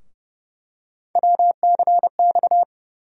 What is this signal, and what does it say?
Morse code, heard as a keyed tone, about 30 wpm, 700 Hz. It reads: WCX
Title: USA Call Sign Postfixes - 3 Letters 30wpm